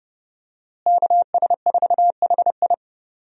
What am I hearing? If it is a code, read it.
Morse code, heard as a keyed tone, about 29 wpm, 700 Hz. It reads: KS4HI
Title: USA Call Signs - 5 Characters - Repeated 3x - 30wpm